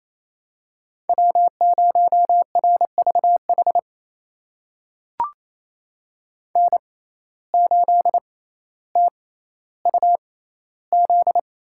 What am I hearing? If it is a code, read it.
Morse code, heard as a keyed tone, about 28 wpm, spaced as for 14 wpm, 700 Hz. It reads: W0RVH N8TUZ
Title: USA Call Signs - 5 Characters - 6x Character Spacing - 28wpm